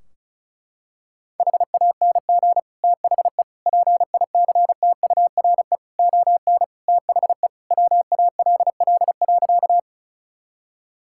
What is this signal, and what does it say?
Morse code, heard as a keyed tone, about 36 wpm, 700 Hz. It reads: HANG THE PICTURE ON THE WALL.
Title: Sentences from Top 1600 Words 35wpm